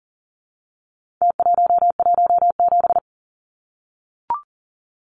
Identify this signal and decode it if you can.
Morse code, heard as a keyed tone, about 40 wpm, 700 Hz. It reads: T117
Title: Numbers - Contest Style - 4-Digit 40wpm